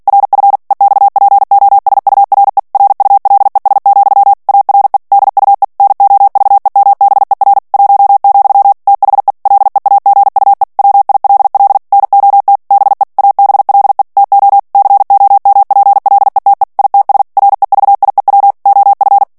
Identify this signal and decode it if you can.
Morse code, heard as a keyed tone, about 48 wpm, 800 Hz. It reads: RF EXPOSURE RULES, ARE DUE NOVEMBER 1, THE LEAGUE WILL NOT BE ABLE TO COMPLETE ITS REVIEW OF
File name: ANGLAIS texte clair 3 vitesse 45 mots minute